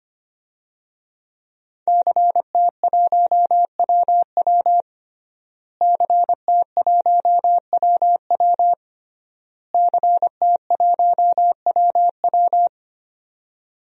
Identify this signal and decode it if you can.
Morse code, heard as a keyed tone, about 25 wpm, 700 Hz. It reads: CT1WW CT1WW CT1WW
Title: DXCC-100 Call Signs - 5 Characters - Repeated 3x - 25wpm